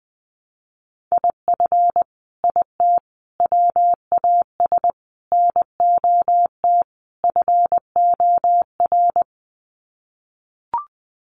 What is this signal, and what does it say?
Morse code, heard as a keyed tone, about 20 wpm, 700 Hz. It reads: IF IT WAS NOT FOR
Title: Sets of 5 Words Encore 20wpm